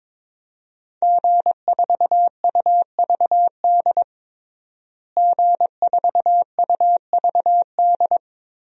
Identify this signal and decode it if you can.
Morse code, heard as a keyed tone, about 22 wpm, 700 Hz. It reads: G4UVD G4UVD
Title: DXCC-100 Call Signs - 5 Characters - Repeated 2x - 22wpm